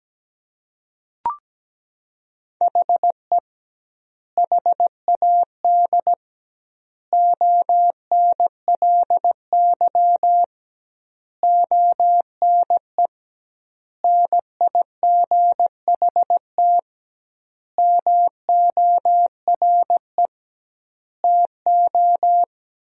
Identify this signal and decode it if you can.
Morse code, heard as a keyed tone, about 17 wpm, 700 Hz. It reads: HE HAD ONLY ONE NIGHT MORE TO
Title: 2x Word-Spacing - Sentences from Top 300 Words - 17wpm